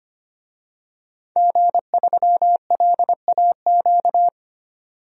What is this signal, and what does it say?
Morse code, heard as a keyed tone, about 25 wpm, 700 Hz. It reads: G3LAQ
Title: DXCC-100 Call Signs - 5 Characters - 25wpm